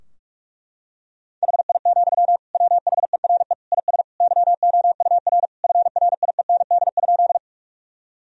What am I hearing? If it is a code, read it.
Morse code, heard as a keyed tone, about 46 wpm, 700 Hz. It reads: HI, WHERE IS YOUR FRIEND?
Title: Sentences from Top 1000 Words - Review Part 3 - 45wpm